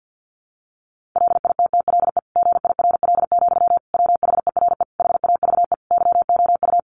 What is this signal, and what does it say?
Morse code, heard as a keyed tone, about 50 wpm, 700 Hz. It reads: LITTLE GIRL, WHERE HAVE YOU
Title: Sentences from Top 400 Words 50wpm